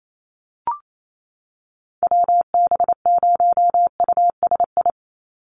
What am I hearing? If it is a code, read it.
Morse code, heard as a keyed tone, about 28 wpm, 700 Hz. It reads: WB0USI
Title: NA SOTA Chaser Callsigns 28wpm